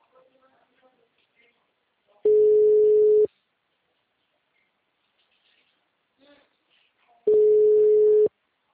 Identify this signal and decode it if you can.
Ringback tone (Europe, 425 Hz), with speech.
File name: out- -2007- - - .42655